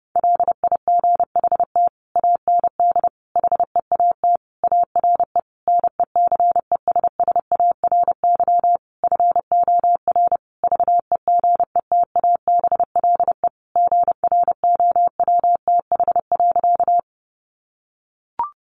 Morse code, heard as a keyed tone, about 30 wpm, 700 Hz. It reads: LIGHT AND HEAT ARE NECESSARY FOR VEGETABLE GROWTH.
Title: Sentences from Top 1300 Words 30wpm